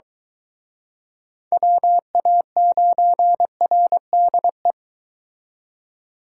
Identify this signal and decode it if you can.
Morse code, heard as a keyed tone, about 23 wpm, 700 Hz. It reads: WA9RDE